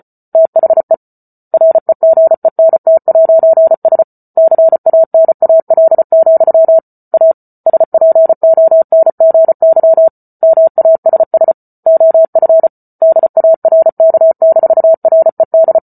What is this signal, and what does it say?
Morse code, heard as a keyed tone, about 34 wpm, 650 Hz. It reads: THE REGENT'S CANAL, A SPONGY MASS OF DARK-RED